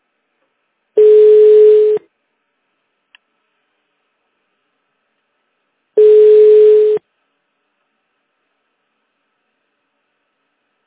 Ringback tone (Europe, 425 Hz).